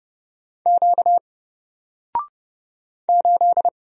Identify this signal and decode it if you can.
Morse code, heard as a keyed tone, about 30 wpm, 700 Hz. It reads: Q 8
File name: Single-Letter-Number-30wpm